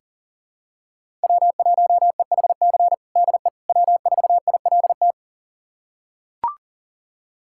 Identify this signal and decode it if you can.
Morse code, heard as a keyed tone, about 40 wpm, 700 Hz. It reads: W1EHC DE W4ILT
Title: USA Callsign DE Callsign - 5 Characters - 40wpm